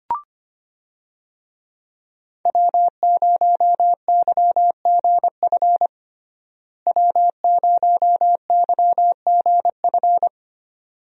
Morse code, heard as a keyed tone, about 25 wpm, 700 Hz. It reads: W0YGF W0YGF
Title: USA Call Signs - 5 Characters - Repeated 2x - 25wpm